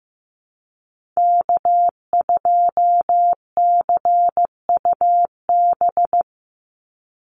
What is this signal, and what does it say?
Morse code, heard as a keyed tone, about 15 wpm, 700 Hz. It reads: K2CUB
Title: Call Signs Repeated 3x - 15wpm